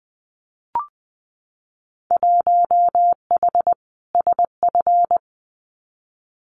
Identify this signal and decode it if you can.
Morse code, heard as a keyed tone, about 20 wpm, 700 Hz. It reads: 1H SF
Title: Winter Field Day Exchanges 20wpm